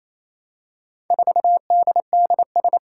Morse code, heard as a keyed tone, about 27 wpm, 700 Hz. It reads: 4DDS
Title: Four Challenging Letter-Number Combinations 28wpm